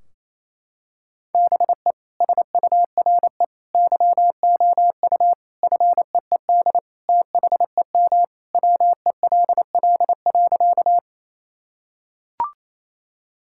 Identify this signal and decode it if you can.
Morse code, heard as a keyed tone, about 28 wpm, 700 Hz. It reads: BE SURE YOU FEED THEM WELL.
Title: Sentences from Top 1600 Words 28wpm